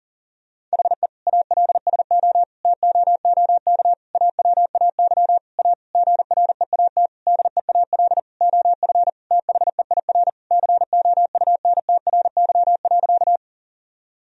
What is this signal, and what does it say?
Morse code, heard as a keyed tone, about 40 wpm, 700 Hz. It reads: HE ALSO TOOK AWAY A GREAT DEAL OF THEIR COUNTRY.
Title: Sentences from Top 1000 Words - Review Part 1 - 40wpm